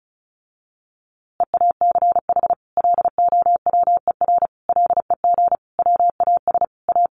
Morse code, heard as a keyed tone, about 34 wpm, 700 Hz. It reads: EACH LOWER LEG WAS A
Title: A Sound Of Thunder 35wpm